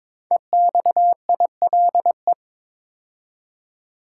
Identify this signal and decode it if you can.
Morse code, heard as a keyed tone, about 22 wpm, 700 Hz. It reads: EXILE
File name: 5 Letter Words - First 3 Letters Given - 22wpm